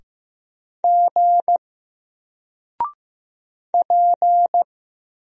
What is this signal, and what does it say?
Morse code, heard as a keyed tone, about 15 wpm, 700 Hz. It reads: G P
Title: Single Letter-Number 15wpm